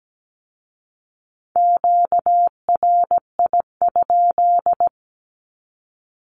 Morse code, heard as a keyed tone, about 17 wpm, 700 Hz. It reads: QRI?
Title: Q-Codes with Spoken Meaning - 17wpm